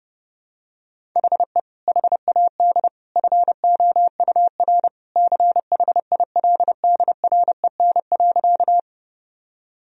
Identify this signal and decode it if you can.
Morse code, heard as a keyed tone, about 32 wpm, 700 Hz. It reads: HE HAD FOUR CHILDREN.
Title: Sentences from Top 1000 Words - Review Part 1 - 30wpm